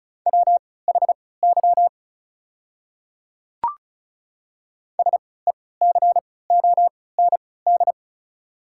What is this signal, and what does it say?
Morse code, heard as a keyed tone, about 34 wpm, spaced as for 20 wpm, 690 Hz. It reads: WHY SECOND
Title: 3x Character Spacing - Top 500 Words - 35wpm